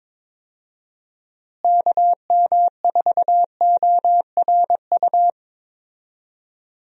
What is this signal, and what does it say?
Morse code, heard as a keyed tone, about 22 wpm, 700 Hz. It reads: KM4ORU
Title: USA Call Signs - 6 Characters - Repeated 2x - 22wpm